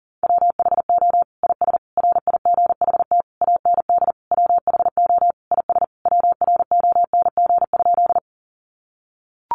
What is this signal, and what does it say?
Morse code, heard as a keyed tone, about 40 wpm, 700 Hz. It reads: WHO IS RIGHT AND WHO IS WRONG?
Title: Sentences from Top 900 Words 40wpm